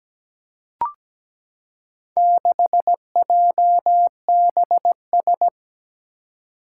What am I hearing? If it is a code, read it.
Morse code, heard as a keyed tone, about 17 wpm, 700 Hz. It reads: 6JBS
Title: Four Challenging Letter-Number Combinations 17wpm